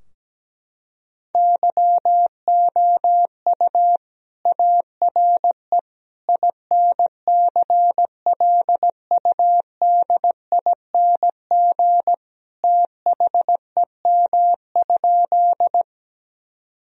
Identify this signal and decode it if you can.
Morse code, heard as a keyed tone, about 17 wpm, 700 Hz. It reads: YOU ARE INCLUDING THEM?
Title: Sentences from Top 1000 Words - Review Part 4 - 17wpm